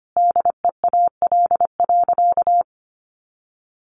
Morse code, heard as a keyed tone, about 25 wpm, 700 Hz. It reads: DEAL.
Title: Sentences from Top 1200 Words 25wpm